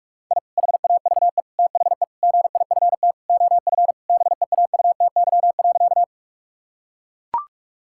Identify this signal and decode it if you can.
Morse code, heard as a keyed tone, about 41 wpm, 700 Hz. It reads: I HAVE THE GIFT OF BEAUTY.
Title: Sentences from Top 1000 Words - Review Part 6 - 45wpm